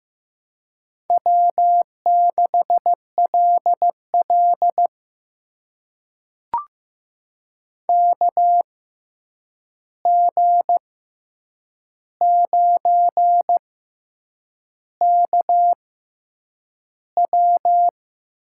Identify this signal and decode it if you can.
Morse code, heard as a keyed tone, about 15 wpm, spaced as for 7 wpm, 700 Hz. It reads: W6LL KG9KW
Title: USA Call Signs - 6x Character Spacing - 15wpm